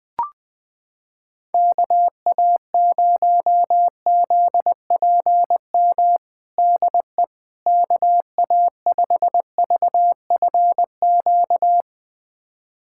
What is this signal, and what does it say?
Morse code, heard as a keyed tone, about 21 wpm, 700 Hz. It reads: KA0ZPM DE KA5VFQ
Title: USA Callsign DE Callsign - 6 Characters - 20wpm